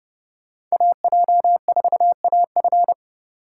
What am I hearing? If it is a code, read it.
Morse code, heard as a keyed tone, about 29 wpm, 700 Hz. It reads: AJ4AF